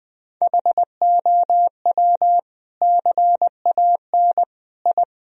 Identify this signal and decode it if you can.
Morse code, heard as a keyed tone, about 20 wpm, 700 Hz. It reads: HOW CAN I
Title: Sentences from Top 1000 Words - Review Part 3 - 20wpm